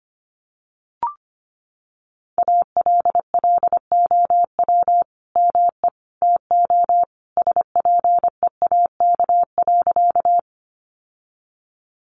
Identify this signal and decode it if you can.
Morse code, heard as a keyed tone, about 25 wpm, 700 Hz. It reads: ALLOW ME TO SPEAK.
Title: Sentences from Top 1100 Words 25wpm